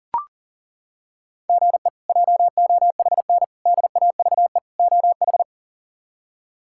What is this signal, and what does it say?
Morse code, heard as a keyed tone, about 40 wpm, 700 Hz. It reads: GE JOHN DAVE OH
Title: K1USN SST QSO Practice - 40wpm